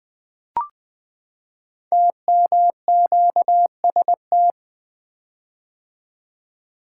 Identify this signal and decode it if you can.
Morse code, heard as a keyed tone, about 20 wpm, 700 Hz. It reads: TMQST
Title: Code Groups - 5 Letter:Numbers - 20wpm